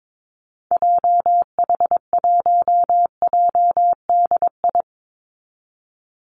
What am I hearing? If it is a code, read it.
Morse code, heard as a keyed tone, about 22 wpm, 700 Hz. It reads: JH1JDI
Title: Call Signs Repeated 2x - 22wpm